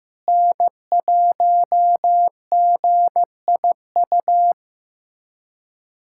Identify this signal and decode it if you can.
Morse code, heard as a keyed tone, about 15 wpm, 700 Hz. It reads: N1GIU
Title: USA Callsign DE Callsign - 5 Characters - 15wpm